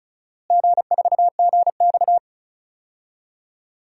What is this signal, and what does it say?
Morse code, heard as a keyed tone, about 36 wpm, 700 Hz. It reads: G4GX
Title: DXCC-100 Call Signs - 4 Characters - Repeated 3x - 35wpm